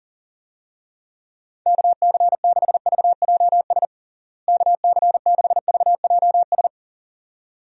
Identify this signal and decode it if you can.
Morse code, heard as a keyed tone, about 40 wpm, 700 Hz. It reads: KC6VJS KC6VJS
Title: USA Call Signs - 6 Characters - Repeated 2x - 40wpm